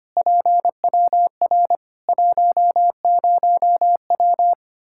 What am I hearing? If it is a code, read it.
Morse code, heard as a keyed tone, about 25 wpm, 700 Hz. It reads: PWR 10W
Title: Practice QSO - Exchange 2 - Info - 25wpm